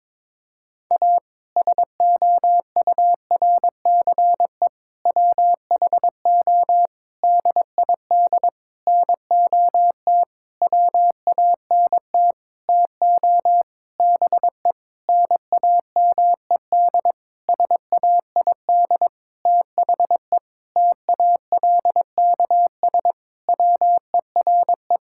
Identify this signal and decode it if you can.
Morse code, heard as a keyed tone, about 22 wpm, 700 Hz. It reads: A SOURCE WHO DID NOT WANT TO BE NAMED SAID THE TALKS WERE
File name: practice / 22wpm / Sentences with 8-Word Phrases - 22wpm